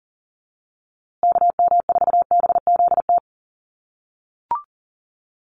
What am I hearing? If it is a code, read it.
Morse code, heard as a keyed tone, about 40 wpm, 700 Hz. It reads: KM4BZT
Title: USA Call Signs Repeated 2x 40wpm